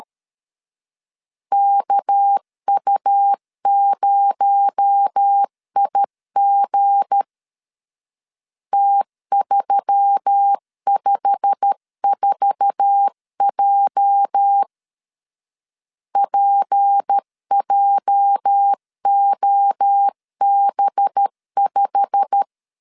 Morse code, heard as a keyed tone, about 13 wpm, 800 Hz. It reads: KU0IG T354J PJOB5
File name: LEZIONI / 8 YJ45 / 8 YJ45 12wpm 2151